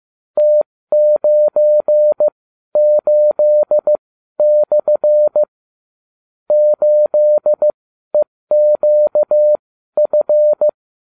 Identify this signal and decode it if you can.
Morse code, heard as a keyed tone, about 15 wpm, spaced as for 12 wpm, 600 Hz. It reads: T98/ 8EQF